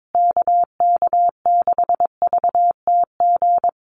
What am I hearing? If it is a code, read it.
Morse code, heard as a keyed tone, about 22 wpm, 700 Hz. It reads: KK6VTG
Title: USA Call Signs Repeated 2x 22wpm